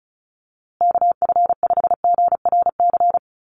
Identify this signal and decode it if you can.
Morse code, heard as a keyed tone, about 34 wpm, 700 Hz. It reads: KF5GRC
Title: USA Call Signs Repeated 2x 35wpm